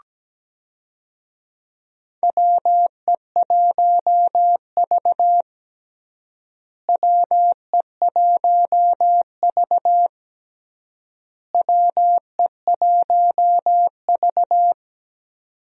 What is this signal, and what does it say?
Morse code, heard as a keyed tone, about 17 wpm, 700 Hz. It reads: WE1V WE1V WE1V
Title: USA Call Signs - 4 Characters - Repeated 3x - 17wpm